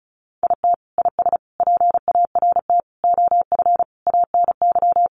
Morse code, heard as a keyed tone, about 34 wpm, 700 Hz. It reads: IT IS PART OF ANY
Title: Sentences from Top 1100 Words 35wpm